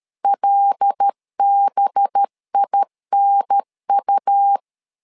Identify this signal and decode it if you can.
Morse code, heard as a keyed tone, about 13 wpm, 800 Hz. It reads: LBINU
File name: LEZIONI / 5 LBVC / 5 LB 12wpm 1906